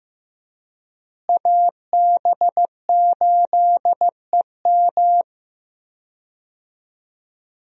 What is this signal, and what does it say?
Morse code, heard as a keyed tone, about 15 wpm, 700 Hz. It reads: AB8EM
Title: USA Call Signs - 5 Characters - Repeated 2x - 15wpm